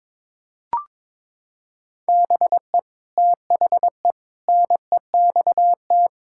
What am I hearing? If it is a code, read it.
Morse code, heard as a keyed tone, about 22 wpm, 700 Hz. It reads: BE THE NEXT
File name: practice / 22wpm / Sets of 8 Words - 22wpm